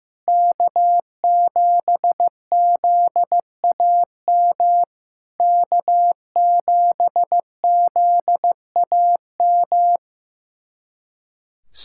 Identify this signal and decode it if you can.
Morse code, heard as a keyed tone, about 16 wpm, 700 Hz. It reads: K7ZAM K7ZAM
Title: SOTA QSO Practice 15wpm